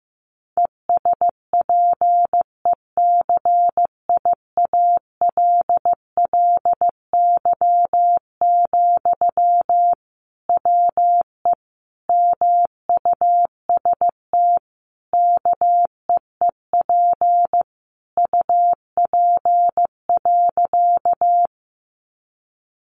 Morse code, heard as a keyed tone, about 15 wpm, 700 Hz. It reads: ESPECIALLY, WE MUST KEEP UP.
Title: Sentences from Top 700 Words 15wpm